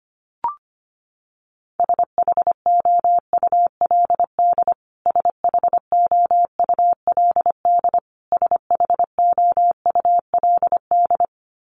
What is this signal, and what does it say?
Morse code, heard as a keyed tone, about 25 wpm, 700 Hz. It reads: SHOULD SHOULD SHOULD
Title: Top 500 Words Repeated 25wpm